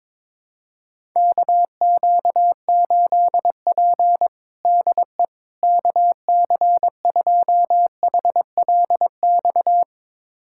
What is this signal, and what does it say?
Morse code, heard as a keyed tone, about 22 wpm, 700 Hz. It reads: KQ8P DE KC2HLX
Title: USA Callsign DE Callsign 22wpm